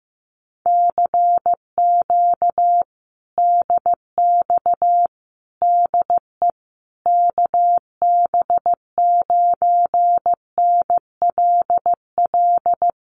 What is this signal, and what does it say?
Morse code, heard as a keyed tone, about 15 wpm, 700 Hz. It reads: CQ DX DE KB9NLL
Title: Common QSO Phrases Encore 15wpm